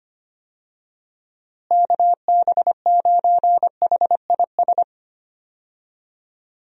Morse code, heard as a keyed tone, about 25 wpm, 700 Hz. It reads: KB9HIS
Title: USA Call Signs - 6 Characters - Repeated 2x - 25wpm